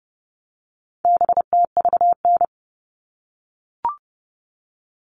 Morse code, heard as a keyed tone, about 30 wpm, 700 Hz. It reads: BTVN
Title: All Cut Numbers - 4-Digit 30wpm